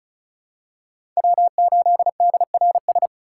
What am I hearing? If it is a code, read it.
Morse code, heard as a keyed tone, about 36 wpm, 700 Hz. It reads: W8DRS
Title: USA Call Signs - 5 Characters - 3x Character Spacing - 35wpm